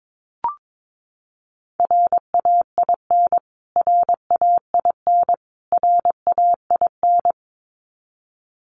Morse code, heard as a keyed tone, about 22 wpm, 700 Hz. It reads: RAIN RAIN RAIN
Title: Top 500 Words Repeated 22wpm